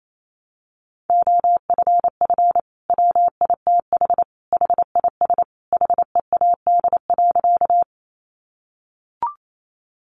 Morse code, heard as a keyed tone, about 28 wpm, 700 Hz. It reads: OFF WITH HIS HEAD.
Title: Sentences from Top 300 Words 28wpm